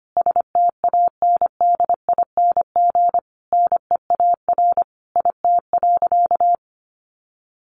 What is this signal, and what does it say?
Morse code, heard as a keyed tone, about 25 wpm, 700 Hz. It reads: STANDING NEAR IT.
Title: A Sound Of Thunder 25wpm